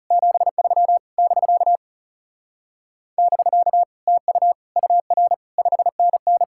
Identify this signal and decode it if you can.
Morse code, heard as a keyed tone, about 38 wpm, 700 Hz. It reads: 73 <BK> <BK> TU UR 5NN